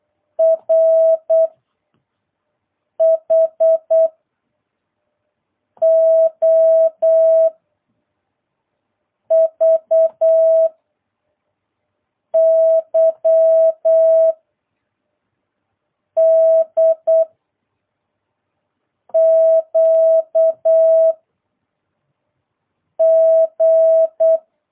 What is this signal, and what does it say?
Morse code, heard as a keyed tone, about 8 wpm, 650 Hz. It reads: R H O V Y D Q G